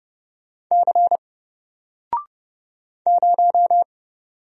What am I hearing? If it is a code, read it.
Morse code, heard as a keyed tone, about 30 wpm, 700 Hz. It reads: C 0
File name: Single-Letter-Number-30wpm